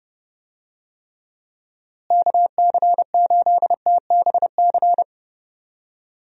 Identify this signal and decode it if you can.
Morse code, heard as a keyed tone, about 30 wpm, 700 Hz. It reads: KC8TBC
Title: USA Call Signs - 6 Characters - Repeated 2x - 30wpm